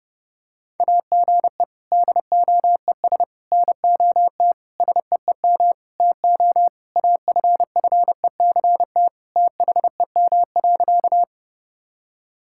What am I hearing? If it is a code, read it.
Morse code, heard as a keyed tone, about 32 wpm, 700 Hz. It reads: AGE DOES NOT SEEM TO AFFECT THEM.
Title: Sentences from Top 1000 Words - Review Part 6 - 30wpm